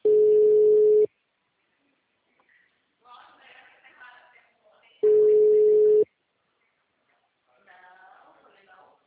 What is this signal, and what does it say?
Ringback tone (Europe, 425 Hz), with speech.